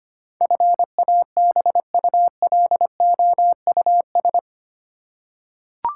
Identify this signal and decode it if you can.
Morse code, heard as a keyed tone, about 25 wpm, 700 Hz. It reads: FABULOUS
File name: Eight Letter Words - 25wpm